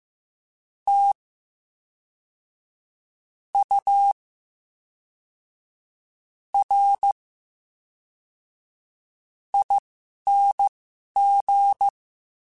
Morse code, heard as a keyed tone, about 15 wpm, spaced as for 11 wpm, 800 Hz. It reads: T U R ING
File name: Special INGs / Special INGs-mp3s / VerySlowToFaster Ing @10